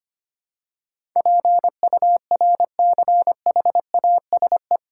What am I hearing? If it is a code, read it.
Morse code, heard as a keyed tone, about 25 wpm, 700 Hz. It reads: PURCHASE
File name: Eight Letter Words - 25wpm